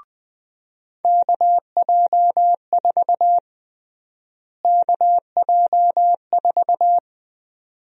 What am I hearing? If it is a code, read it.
Morse code, heard as a keyed tone, about 20 wpm, 700 Hz. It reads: KJ4 KJ4
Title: USA Call Sign Prefixes - Repeated 2x - 20wpm